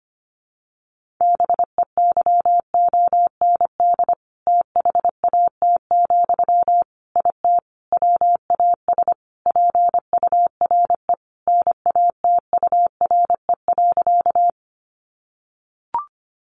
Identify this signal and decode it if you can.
Morse code, heard as a keyed tone, about 25 wpm, 700 Hz. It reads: BEYOND THAT, IT WAS PURE NATURE.
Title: Sentences from Top 1500 Words 25wpm